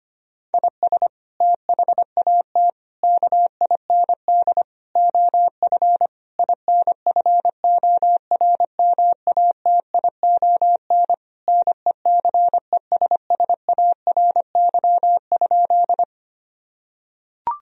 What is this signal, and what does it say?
Morse code, heard as a keyed tone, about 25 wpm, 700 Hz. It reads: IS THAT KIND OF INFORMATION NECESSARY?
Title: Sentences from Top 1000 Words - Review Part 3 - 25wpm